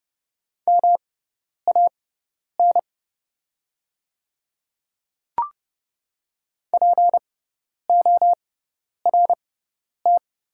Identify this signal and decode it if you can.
Morse code, heard as a keyed tone, about 30 wpm, spaced as for 10 wpm, 700 Hz. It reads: MAN PORT
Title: 6x Character Spacing - Top 500 Words - 30wpm